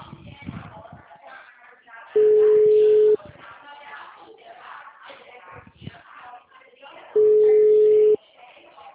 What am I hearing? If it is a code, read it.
Ringback tone (Europe, 425 Hz), with speech.